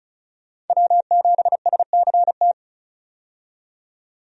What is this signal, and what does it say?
Morse code, heard as a keyed tone, about 34 wpm, 700 Hz. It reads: W7SCT
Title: USA Call Signs - 5 Characters - 35wpm